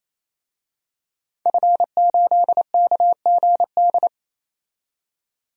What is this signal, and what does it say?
Morse code, heard as a keyed tone, about 28 wpm, 700 Hz. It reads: F8KGD
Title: DXCC-100 Call Signs - 5 Characters - 28wpm